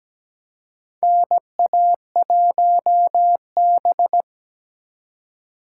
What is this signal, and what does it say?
Morse code, heard as a keyed tone, about 17 wpm, 700 Hz. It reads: NA1B
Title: USA Call Signs - 4 Characters - 2x Character Spacing - 17wpm